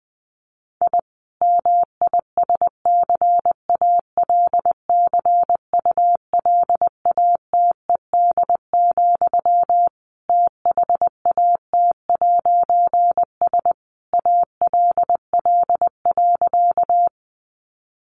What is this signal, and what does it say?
Morse code, heard as a keyed tone, about 20 wpm, 700 Hz. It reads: I MISCALCULATED, THAT'S ALL.
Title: A Sound Of Thunder 20wpm